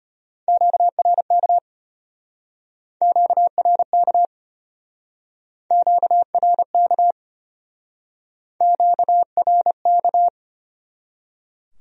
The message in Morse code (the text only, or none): QRK QRK QRK QRK